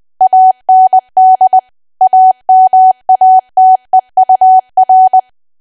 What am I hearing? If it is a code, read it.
Morse code, heard as a keyed tone, about 20 wpm, 750 Hz. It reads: AND AMATEUR
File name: cwa / W1AW-Level3 / 20WPM